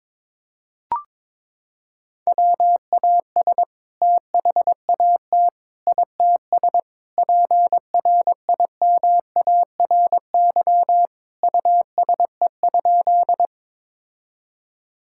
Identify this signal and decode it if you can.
Morse code, heard as a keyed tone, about 22 wpm, 700 Hz. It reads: WAS THAT ITS PRIMARY USE?
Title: Sentences from Top 1000 Words - Review Part 5 - 22wpm